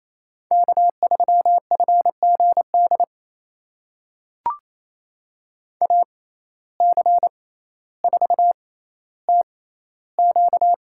Morse code, heard as a keyed tone, about 28 wpm, spaced as for 14 wpm, 700 Hz. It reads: K3FGD AC4TQ
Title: USA Call Signs - 5 Characters - 6x Character Spacing - 28wpm